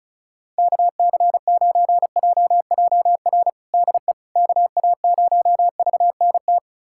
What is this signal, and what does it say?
Morse code, heard as a keyed tone, about 34 wpm, 700 Hz. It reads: KC9JJR DE KA0VNT
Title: USA Callsign DE Callsign - 6 Characters - 35wpm